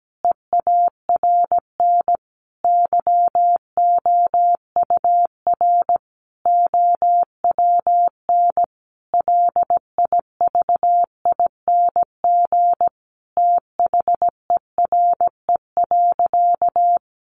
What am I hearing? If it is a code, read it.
Morse code, heard as a keyed tone, about 17 wpm, 700 Hz. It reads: EARN YOUR OWN LIVING THERE.
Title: Sentences from Top 1500 Words 17wpm